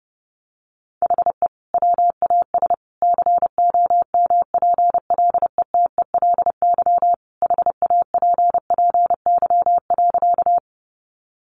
Morse code, heard as a keyed tone, about 30 wpm, 700 Hz. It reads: HE WAS COMPLETELY HAPPY.
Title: Sentences from Top 1100 Words 30wpm